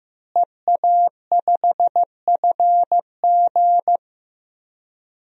Morse code, heard as a keyed tone, about 15 wpm, 700 Hz. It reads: EA5FG
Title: DXCC-100 Call Signs - 5 Characters - Repeated 2x - 15wpm